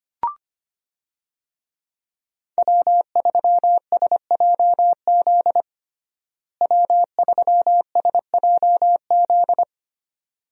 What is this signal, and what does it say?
Morse code, heard as a keyed tone, about 25 wpm, 700 Hz. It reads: W3SJZ W3SJZ
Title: USA Call Signs - 5 Characters - Repeated 2x - 25wpm